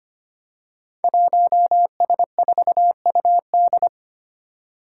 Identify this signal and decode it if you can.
Morse code, heard as a keyed tone, about 25 wpm, 700 Hz. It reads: 1S4UD
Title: Five Challenging Letter-Number Combinations 25wpm